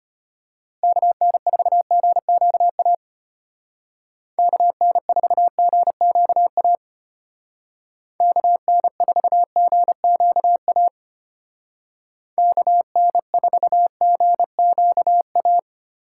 Morse code, heard as a keyed tone, about 30 wpm, 700 Hz. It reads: KN4GQA KN4GQA KN4GQA KN4GQA
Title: Speed Racing - USA 2x3 Call Signs 25wpm